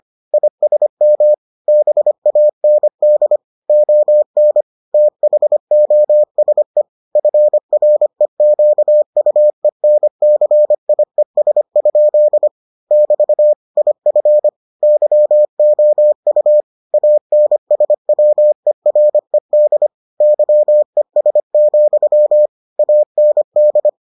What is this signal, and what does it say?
Morse code, heard as a keyed tone, about 25 wpm, 600 Hz. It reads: ISM BAND ON THOSE FREQUENCIES? <BT> IF YOU ANSWERED YES, AND